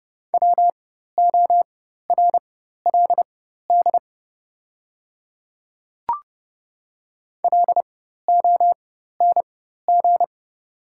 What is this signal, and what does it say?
Morse code, heard as a keyed tone, about 29 wpm, spaced as for 14 wpm, 700 Hz. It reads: WORLD LONG
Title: 4x Character Spacing - Top 300 Words - 30wpm